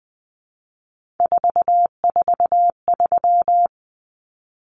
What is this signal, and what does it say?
Morse code, heard as a keyed tone, about 20 wpm, 700 Hz. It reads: 443
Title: Numbers - 3-Digit 20wpm